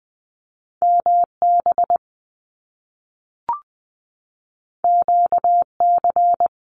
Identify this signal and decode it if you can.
Morse code, heard as a keyed tone, about 20 wpm, 700 Hz. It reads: MB QC
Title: Canadian Provinces and Territories 20wpm